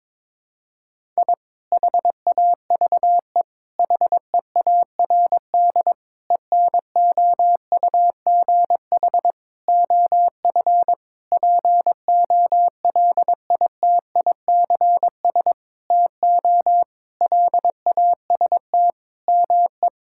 Morse code, heard as a keyed tone, about 22 wpm, 700 Hz. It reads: I HAVE HEARD ENOUGH OF POLITICS TO LAST ME
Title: Sentences from Top 1000 Words - Review Part 6 - 22wpm